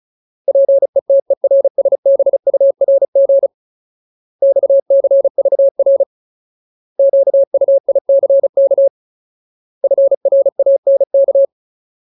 Morse code, heard as a keyed tone, about 34 wpm, 550 Hz. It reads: PETERSBURG XCVR QUICK FRANK
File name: Ultimate Word List / 35WPM Ultimate